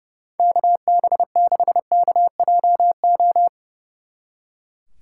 Morse code, heard as a keyed tone, about 30 wpm, 700 Hz. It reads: KB6KJO
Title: Speed Racing - USA 2x3 Call Signs 30wpm